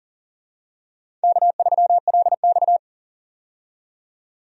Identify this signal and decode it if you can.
Morse code, heard as a keyed tone, about 40 wpm, 700 Hz. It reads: K3LX
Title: USA Call Signs - 4 Characters - 40wpm